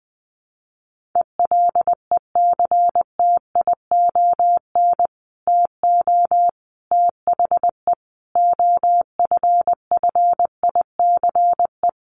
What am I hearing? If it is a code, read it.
Morse code, heard as a keyed tone, about 20 wpm, 700 Hz. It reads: ELECTION TO THE OFFICE
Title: Sentences from Top 1500 Words 20wpm